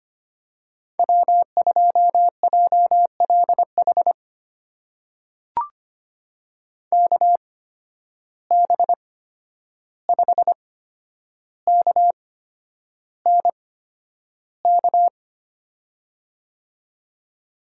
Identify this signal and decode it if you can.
Morse code, heard as a keyed tone, about 25 wpm, spaced as for 10 wpm, 700 Hz. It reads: W2JLH KB5KNK
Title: USA Call Signs - 8x Character Spacing - 25wpm